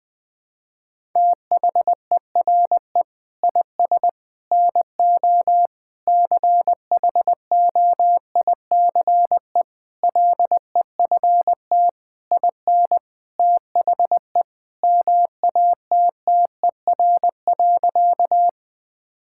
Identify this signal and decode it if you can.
Morse code, heard as a keyed tone, about 19 wpm, 700 Hz. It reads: THERE IS NO CHOICE LEFT IN THE MATTER.
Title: Sentences from Top 1000 Words - Review Part 6 - 20wpm